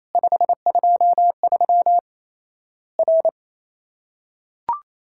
Morse code, heard as a keyed tone, about 28 wpm, 700 Hz. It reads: 523 R
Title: Worked All Europe DX Contest - QTC Exchange - 28wpm